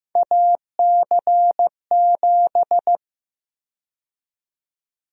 Morse code, heard as a keyed tone, about 16 wpm, 700 Hz. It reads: AC7
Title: USA Call Sign Prefixes - Repeated 2x - 15wpm